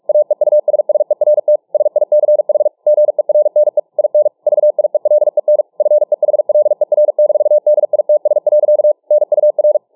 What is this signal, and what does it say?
Morse code, heard as a keyed tone, about 46 wpm, 600 Hz. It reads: AEUSSERT SICH GERNE IN VIELEN FEHLER-DITS. NUR